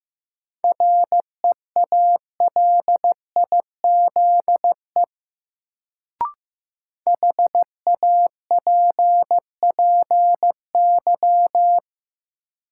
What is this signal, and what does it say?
Morse code, heard as a keyed tone, about 15 wpm, 700 Hz. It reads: REALIZE HAPPY
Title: Sentences from Top 1000 Words - New Word Drill - 15wpm